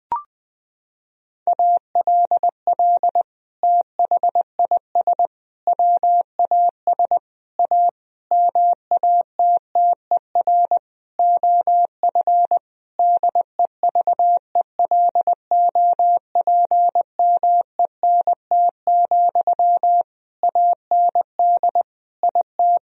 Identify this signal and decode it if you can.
Morse code, heard as a keyed tone, about 21 wpm, 700 Hz. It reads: ALL THIS WAS A MATTER OF DEVELOPMENT, AND IT
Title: Sentences from Top 1000 Words - Review Part 4 - 20wpm